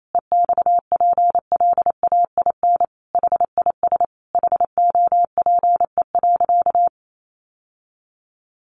Morse code, heard as a keyed tone, about 28 wpm, 700 Hz. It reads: EXPLAIN HIS HOPE.
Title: Sentences from Top 1100 Words 28wpm